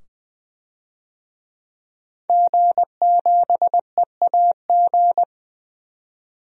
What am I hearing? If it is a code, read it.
Morse code, heard as a keyed tone, about 20 wpm, 700 Hz. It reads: G7EAG
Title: DXCC-100 Call Signs - 5 Characters - 20wpm